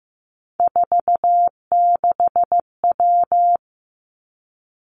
Morse code, heard as a keyed tone, about 15 wpm, 700 Hz. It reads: 46W
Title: Three Challenging Letter-Number Combinations 15wpm